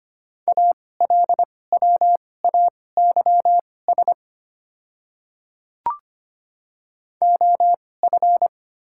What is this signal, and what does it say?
Morse code, heard as a keyed tone, about 25 wpm, spaced as for 18 wpm, 690 Hz. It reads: ALWAYS OF
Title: 2x Character Spacing - Top 300 Words - 25wpm